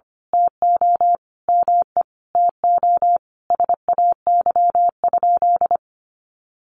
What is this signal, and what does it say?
Morse code, heard as a keyed tone, about 25 wpm, 700 Hz. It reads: TO ME TO SAY?
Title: Sentences from Top 200 Words 25wpm